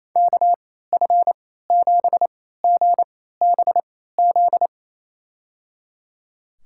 Morse code, heard as a keyed tone, about 28 wpm, spaced as for 16 wpm, 700 Hz. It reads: KF7GBZ
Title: USA Call Signs - 6 Characters - 3x Character Spacing - 28wpm